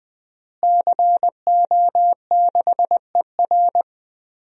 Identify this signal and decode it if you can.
Morse code, heard as a keyed tone, about 20 wpm, 700 Hz. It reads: CO6ER
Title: DXCC-100 Call Signs - 5 Characters - Repeated 3x - 20wpm